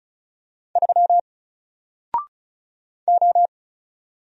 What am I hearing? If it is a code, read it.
Morse code, heard as a keyed tone, about 36 wpm, 700 Hz. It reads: 3 O
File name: Single-Letter-Number-35wpm